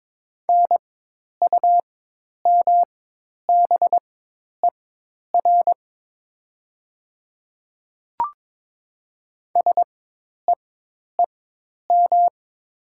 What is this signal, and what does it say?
Morse code, heard as a keyed tone, about 22 wpm, spaced as for 10 wpm, 700 Hz. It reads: NUMBER SEEM
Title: 4x Character Spacing - Top 500 Words - 22wpm